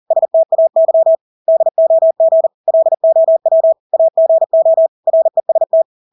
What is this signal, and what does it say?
Morse code, heard as a keyed tone, about 40 wpm, 650 Hz. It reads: STAY DOG LOW AGO REST
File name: Engelske ord 40-40 wpm